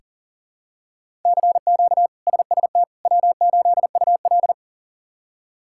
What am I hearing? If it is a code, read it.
Morse code, heard as a keyed tone, about 40 wpm, 700 Hz. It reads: CQ SST W8UL
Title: K1USN SST QSO Practice - 40wpm